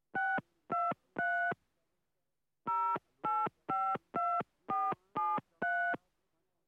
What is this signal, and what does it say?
DTMF, heard as telephone touch tones, tones of about 250 ms. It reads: 623 0852473